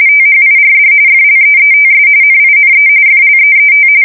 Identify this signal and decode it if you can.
RTTY, about 45.45 baud. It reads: THIRD FLEETS NINE FLEET
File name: fsk170 / USS-New-Jersey